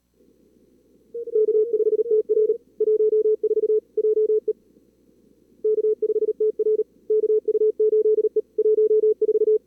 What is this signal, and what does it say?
Morse code, heard as a keyed tone, about 38 wpm, 440 Hz. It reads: K5TR 14 JE K5TR KU8E 14
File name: CWSprint / Feb2022 / k5tr-feb-na-sprint-cw-k5ot -2300